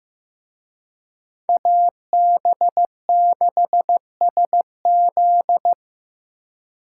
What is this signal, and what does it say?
Morse code, heard as a keyed tone, about 18 wpm, 700 Hz. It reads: AB6SZ
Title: Speed Racing - USA 2x2 Call Signs 15wpm